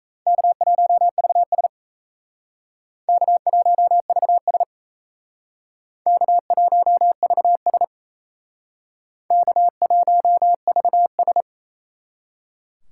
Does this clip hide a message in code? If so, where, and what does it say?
Yes: Morse code from 0.3 to 11.4 s, heard as a keyed tone, about 35 wpm, 700 Hz. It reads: K1VS K1VS K1VS K1VS